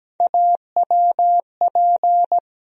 Morse code, heard as a keyed tone, about 17 wpm, 700 Hz. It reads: AWP
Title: USA Call Sign Postfixes - 3 Letters 17wpm